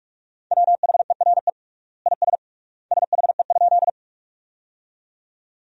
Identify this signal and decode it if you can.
Morse code, heard as a keyed tone, about 46 wpm, 700 Hz. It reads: WHERE IS SHE?
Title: 3x Word-Spacing - Sentences from Top 200 Words - 45wpm